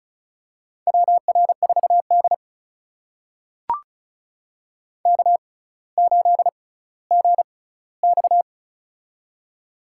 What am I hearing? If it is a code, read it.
Morse code, heard as a keyed tone, about 35 wpm, spaced as for 18 wpm, 700 Hz. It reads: WR4D K8GX
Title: USA Call Signs - 4 Characters - 6x Character Spacing - 35wpm